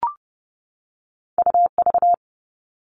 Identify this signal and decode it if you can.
Morse code, heard as a keyed tone, about 30 wpm, 700 Hz. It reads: U V
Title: All Cut Numbers - 2-Digit 30wpm